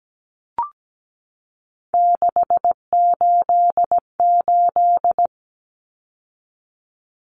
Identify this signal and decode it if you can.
Morse code, heard as a keyed tone, about 17 wpm, 700 Hz. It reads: 688
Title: Numbers - 3-Digit 17wpm